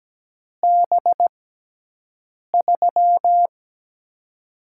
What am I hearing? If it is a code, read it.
Morse code, heard as a keyed tone, about 17 wpm, spaced as for 6 wpm, 700 Hz. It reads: B3
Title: USA Call Signs - 6 Characters - 6x Character Spacing - 17wpm